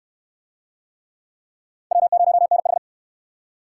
Morse code, heard as a keyed tone, about 66 wpm, 700 Hz. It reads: W0NH